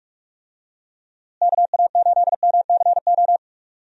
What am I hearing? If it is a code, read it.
Morse code, heard as a keyed tone, about 46 wpm, 700 Hz. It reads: KA8MCO
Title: USA Call Signs - 6 Characters - Repeated 3x - 45wpm